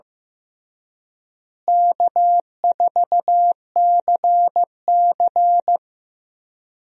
Morse code, heard as a keyed tone, about 15 wpm, 700 Hz. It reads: K4CC
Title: USA Call Signs - 4 Characters - Repeated 3x - 15wpm